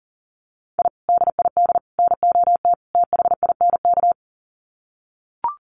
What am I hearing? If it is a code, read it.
Morse code, heard as a keyed tone, about 40 wpm, 700 Hz. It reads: I DID NOT THINK
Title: Sets of 4 Words 40wpm